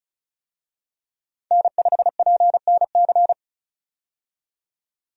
Morse code, heard as a keyed tone, about 36 wpm, 700 Hz. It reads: N5PNC
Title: USA Call Signs - 5 Characters - 35wpm